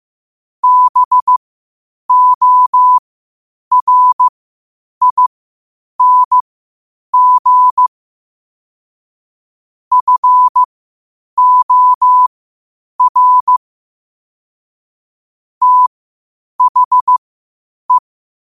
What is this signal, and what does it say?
Morse code, heard as a keyed tone, about 15 wpm, spaced as for 8 wpm, 1000 Hz. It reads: BORING FOR THE
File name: qotdfiles / 8wpm